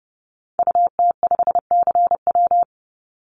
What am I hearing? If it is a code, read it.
Morse code, heard as a keyed tone, about 30 wpm, 700 Hz. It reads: UT5CW
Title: Call Signs Repeated 2x - 30wpm